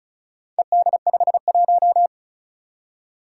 Morse code, heard as a keyed tone, about 34 wpm, 700 Hz. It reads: ED51
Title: Four Challenging Letter-Number Combinations 35wpm